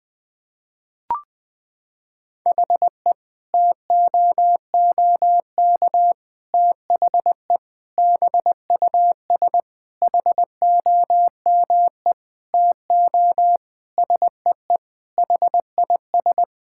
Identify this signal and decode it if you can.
Morse code, heard as a keyed tone, about 20 wpm, 700 Hz. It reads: HE TOOK THE BUS HOME TO SEE HIS
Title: Sentences from Top 1000 Words - Review Part 6 - 20wpm